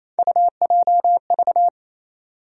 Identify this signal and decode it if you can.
Morse code, heard as a keyed tone, about 28 wpm, 700 Hz. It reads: UJV
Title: USA Call Sign Postfixes - 3 Letters 28wpm